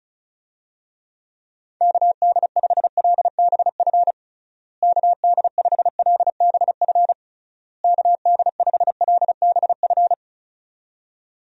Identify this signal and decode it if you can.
Morse code, heard as a keyed tone, about 36 wpm, 700 Hz. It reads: KD5LBF KD5LBF KD5LBF
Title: USA Call Signs - 6 Characters - Repeated 3x - 35wpm